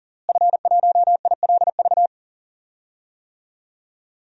Morse code, heard as a keyed tone, about 40 wpm, 700 Hz. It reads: F1ILV
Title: DXCC-100 Call Signs - 5 Characters - Repeated 3x - 40wpm